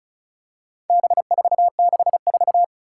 Morse code, heard as a keyed tone, about 35 wpm, 700 Hz. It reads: B464
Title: Four Challenging Letter-Number Combinations 35wpm